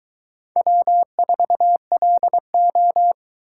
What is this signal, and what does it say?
Morse code, heard as a keyed tone, about 22 wpm, 700 Hz. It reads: W4LO